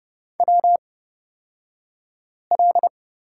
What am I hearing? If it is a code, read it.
Morse code, heard as a keyed tone, about 30 wpm, spaced as for 5 wpm, 700 Hz. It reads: WL
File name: Lesson 36d - BT - Two Characters 30wpm